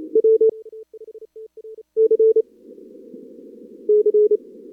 Morse code, heard as a keyed tone, about 34 wpm, 440 Hz. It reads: W C C
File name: CWSprint / Feb2022 / k5tr-feb-na-sprint-cw-k5ot -0100